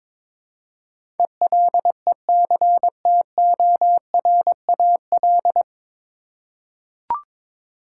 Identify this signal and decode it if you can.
Morse code, heard as a keyed tone, about 21 wpm, 700 Hz. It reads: ELECTORAL
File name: 9 Letter Words - First 3 Letters Given - 22wpm